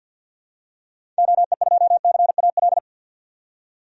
Morse code, heard as a keyed tone, about 51 wpm, 700 Hz. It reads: OE2CAL
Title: DXCC-100 Call Signs - 6 Characters - 50wpm